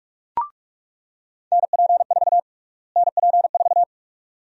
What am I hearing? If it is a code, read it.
Morse code, heard as a keyed tone, about 46 wpm, 690 Hz. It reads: NP4 NP4
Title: USA Call Sign Prefixes - Repeated 2x - 45wpm